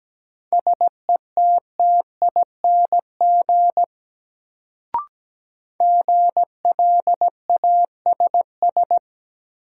Morse code, heard as a keyed tone, about 17 wpm, 700 Hz. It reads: SETTING GLASS
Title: Sentences from Top 1300 Words - New Word Drill Alt - 17wpm